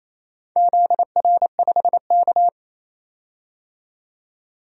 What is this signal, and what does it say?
Morse code, heard as a keyed tone, about 28 wpm, 700 Hz. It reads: ZR5K
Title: DXCC-100 Call Signs - 4 Characters - 28wpm